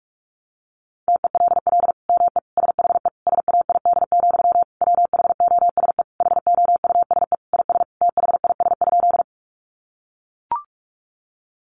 Morse code, heard as a keyed tone, about 44 wpm, 700 Hz. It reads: TELL ME SHE SAID, WHOSE HOUSE IS THIS?
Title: Sentences from Top 1200 Words 45wpm